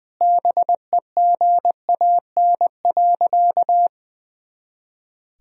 Morse code, heard as a keyed tone, about 20 wpm, 700 Hz. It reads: BEGAN.
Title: Sentences from Top 400 Words and Abbreviations 20wpm